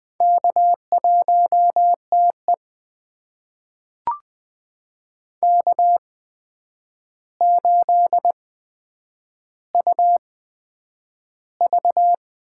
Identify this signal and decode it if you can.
Morse code, heard as a keyed tone, about 20 wpm, spaced as for 9 wpm, 700 Hz. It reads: K1TE K8UV
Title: USA Call Signs - 4 Characters - 8x Character Spacing - 20wpm